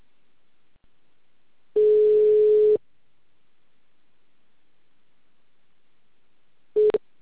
Ringback tone (Europe, 425 Hz).